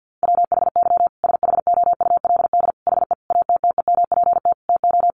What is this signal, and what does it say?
Morse code, heard as a keyed tone, about 50 wpm, 700 Hz. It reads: WHY SHOULD HE ATTEMPT TO
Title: Sentences from Top 1400 Words 50wpm